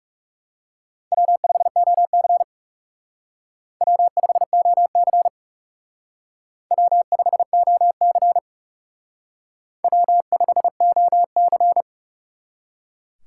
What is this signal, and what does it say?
Morse code, heard as a keyed tone, about 37 wpm, 700 Hz. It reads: W5OC W5OC W5OC W5OC
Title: Speed Racing - USA 1x2 Call Signs 30wpm